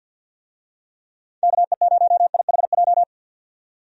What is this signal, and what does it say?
Morse code, heard as a keyed tone, about 50 wpm, 700 Hz. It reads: KE0IHJ
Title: USA Call Signs - 6 Characters - 50wpm